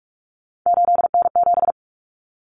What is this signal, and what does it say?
Morse code, heard as a keyed tone, about 44 wpm, 700 Hz. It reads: 8N7
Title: Common Cut Numbers - 3-Digit 45wpm